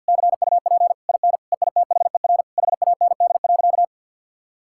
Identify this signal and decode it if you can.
Morse code, heard as a keyed tone, about 50 wpm, 700 Hz. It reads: CUP IN EITHER HAND.
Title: Sentences from Top 1600 Words 50wpm